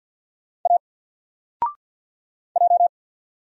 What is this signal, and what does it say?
Morse code, heard as a keyed tone, about 47 wpm, 700 Hz. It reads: A J
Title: Single Letter-Number 50wpm